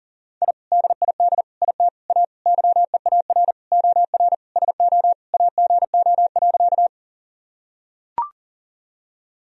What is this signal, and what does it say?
Morse code, heard as a keyed tone, about 40 wpm, 700 Hz. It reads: I DID IT A YEAR OR SO AGO.
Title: Sentences from Top 1000 Words - Review Part 3 - 40wpm